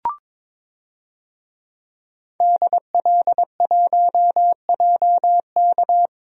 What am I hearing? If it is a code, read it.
Morse code, heard as a keyed tone, about 22 wpm, 700 Hz. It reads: DL1JK
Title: DXCC-100 Call Signs - 5 Characters - 22wpm